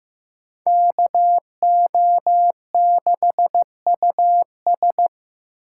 Morse code, heard as a keyed tone, about 15 wpm, 700 Hz. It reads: KO6US
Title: USA Call Signs - 5 Characters - 4x Character Spacing - 15wpm